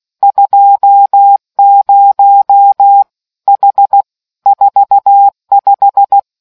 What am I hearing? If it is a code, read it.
Morse code, heard as a keyed tone, about 16 wpm, 800 Hz. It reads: 20 H 45